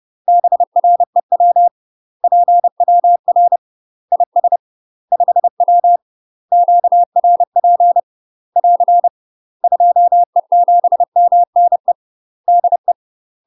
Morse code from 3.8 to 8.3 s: IS 5W QRP